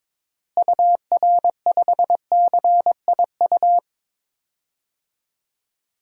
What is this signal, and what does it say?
Morse code, heard as a keyed tone, about 22 wpm, 700 Hz. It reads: UR5CIU
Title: DXCC-100 Call Signs - 6 Characters - Repeated 3x - 22wpm